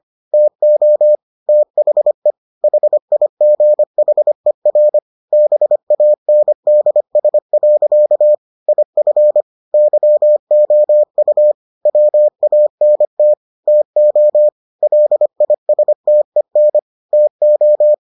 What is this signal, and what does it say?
Morse code, heard as a keyed tone, about 25 wpm, 600 Hz. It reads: TO THE HIGHER BANDS. IF YOU WANT TO LISTEN TO